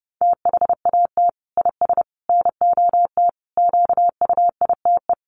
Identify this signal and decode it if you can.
Morse code, heard as a keyed tone, about 30 wpm, 700 Hz. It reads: THAT IS NOT QUITE
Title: Sentences from Top 700 Words 30wpm